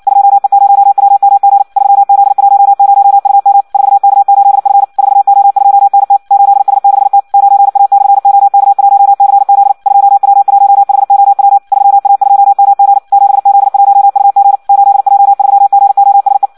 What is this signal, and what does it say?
Morse code, heard as a keyed tone, about 56 wpm, 800 Hz. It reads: 2E0WMG 3Z10UM 4K8F 4O2NT 8S6T 9A/OK1CZ 2W1HOW 3A2MG 7Z1UG 8J3GOSE
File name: Audio practica CW / qrq-tx-58 2